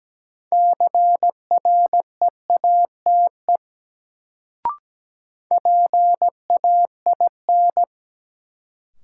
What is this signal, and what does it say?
Morse code, heard as a keyed tone, about 17 wpm, 700 Hz. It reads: CREATE PAIN
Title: Sentences from Top 1600 Words - New Word Drill Alt - 17wpm